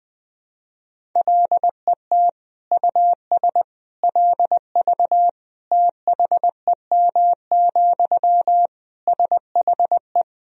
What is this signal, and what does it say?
Morse code, heard as a keyed tone, about 21 wpm, 700 Hz. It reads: LET US LV THEM, SHE
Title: Sentences from Top 400 Words and Abbreviations 20wpm